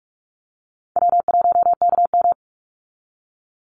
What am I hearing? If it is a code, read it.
Morse code, heard as a keyed tone, about 44 wpm, 700 Hz. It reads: W1KM
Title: Easy Call Signs Repeated 3x - 45wpm